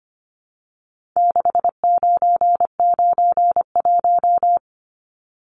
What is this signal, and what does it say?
Morse code, heard as a keyed tone, about 25 wpm, 700 Hz. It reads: 6991
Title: Numbers - 4-Digit 25wpm